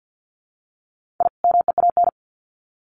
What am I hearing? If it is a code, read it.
Morse code, heard as a keyed tone, about 50 wpm, 700 Hz. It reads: I MEAN
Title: Sets of 2 Words 50wpm